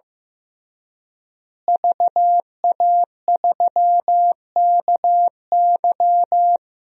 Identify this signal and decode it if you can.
Morse code, heard as a keyed tone, about 16 wpm, 700 Hz. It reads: VA3KY
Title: DXCC-100 Call Signs - 5 Characters - Repeated 3x - 15wpm